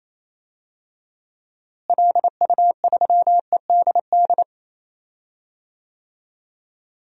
Morse code, heard as a keyed tone, about 27 wpm, 700 Hz. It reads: LU3EDD
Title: DXCC-100 Call Signs - 6 Characters - Repeated 2x - 28wpm